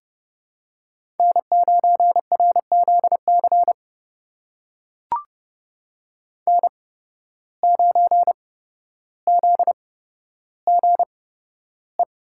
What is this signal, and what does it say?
Morse code, heard as a keyed tone, about 32 wpm, spaced as for 13 wpm, 700 Hz. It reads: N9RZC N9ZGE
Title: USA Call Signs - 8x Character Spacing - 30wpm